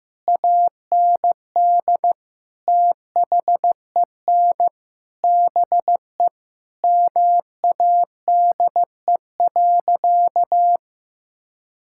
Morse code, heard as a keyed tone, about 15 wpm, 700 Hz. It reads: AND THEN BE MADE.
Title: Sentences from Top 1600 Words 15wpm